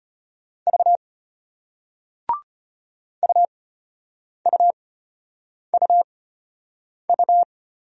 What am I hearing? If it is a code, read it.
Morse code, heard as a keyed tone, about 33 wpm, spaced as for 8 wpm, 700 Hz. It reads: V UUUU